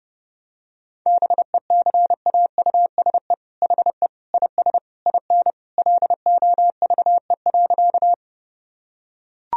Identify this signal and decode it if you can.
Morse code, heard as a keyed tone, about 30 wpm, 700 Hz. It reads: BECAUSE HE IS IN LOVE.
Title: Sentences from Top 1000 Words - Review Part 3 - 30wpm